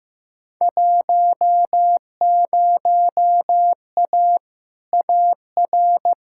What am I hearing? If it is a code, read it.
Morse code, heard as a keyed tone, about 15 wpm, 700 Hz. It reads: 10A AR
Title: Field Day Exchanges Encore - 15wpm